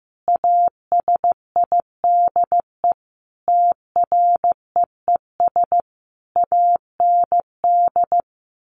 Morse code, heard as a keyed tone, about 15 wpm, 700 Hz. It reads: ASIDE TREES AND
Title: A Sound Of Thunder 15wpm